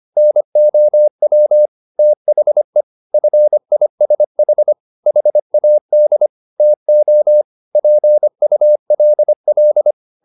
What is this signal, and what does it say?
Morse code, heard as a keyed tone, about 25 wpm, 600 Hz. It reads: NOW THE FISH HAD TO PULL